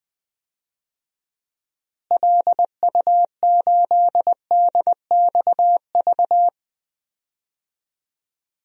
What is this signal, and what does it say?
Morse code, heard as a keyed tone, about 20 wpm, 700 Hz. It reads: LU8DXV
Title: DXCC-100 Call Signs - 6 Characters - Repeated 2x - 20wpm